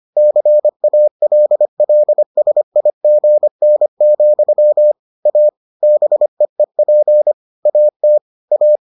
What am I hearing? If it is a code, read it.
Morse code, heard as a keyed tone, about 25 wpm, 600 Hz. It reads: CALLSIGN, A BEEP AT A